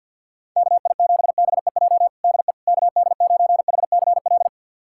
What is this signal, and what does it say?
Morse code, heard as a keyed tone, about 48 wpm, 700 Hz. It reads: KI7BEJ DE KD9HCL
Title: USA Callsign DE Callsign - 6 Characters - 50wpm